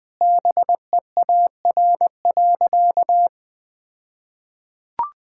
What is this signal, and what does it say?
Morse code, heard as a keyed tone, about 21 wpm, 700 Hz. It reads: BEAR.
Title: All-Age Appropriate Jokes - 20wpm